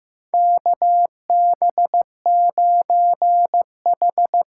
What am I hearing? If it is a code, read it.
Morse code, heard as a keyed tone, about 15 wpm, 700 Hz. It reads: KB9H
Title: USA Call Signs - 6 Characters - 6x Character Spacing - 15wpm